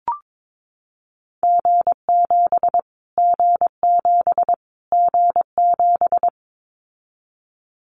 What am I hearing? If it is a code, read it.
Morse code, heard as a keyed tone, about 22 wpm, 700 Hz. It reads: G7 G7 G7
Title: Callsign Prefixes for Top 25 DXCC - Repeated 3x - 22wpm